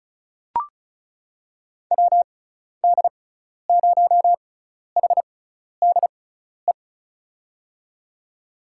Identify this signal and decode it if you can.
Morse code, heard as a keyed tone, about 36 wpm, spaced as for 12 wpm, 690 Hz. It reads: WD0HDE
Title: USA Call Signs - 6 Characters - 6x Character Spacing - 35wpm